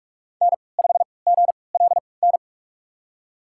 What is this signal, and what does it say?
Morse code, heard as a keyed tone, about 44 wpm, spaced as for 25 wpm, 700 Hz. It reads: N5GLN
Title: USA Call Signs - 5 Characters - 3x Character Spacing - 45wpm